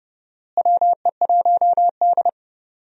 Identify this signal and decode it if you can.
Morse code, heard as a keyed tone, about 30 wpm, 700 Hz. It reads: WE1D
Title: USA Call Signs - 4 Characters - Repeated 3x - 30wpm